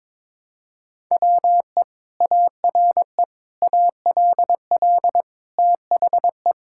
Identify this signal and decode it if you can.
Morse code, heard as a keyed tone, about 22 wpm, 700 Hz. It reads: WE ARE ALL THE
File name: practice / 22wpm / Sentences with 2-Word Phrases - 22wpm